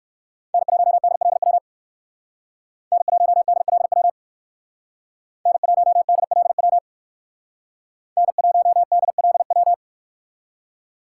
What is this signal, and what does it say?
Morse code, heard as a keyed tone, about 61 wpm, 700 Hz. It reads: N1DLW N1DLW N1DLW N1DLW